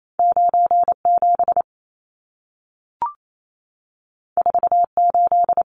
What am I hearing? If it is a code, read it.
Morse code, heard as a keyed tone, about 28 wpm, 700 Hz. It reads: 97 48
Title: Numbers - 2-Digit 28wpm